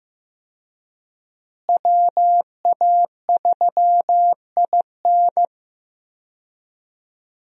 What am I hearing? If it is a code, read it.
Morse code, heard as a keyed tone, about 15 wpm, 700 Hz. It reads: WA3IN
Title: USA Call Signs - 5 Characters - Repeated 2x - 15wpm